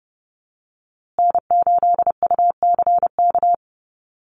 Morse code, heard as a keyed tone, about 30 wpm, 700 Hz. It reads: N8UCK
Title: USA Call Signs Repeated 2x 30wpm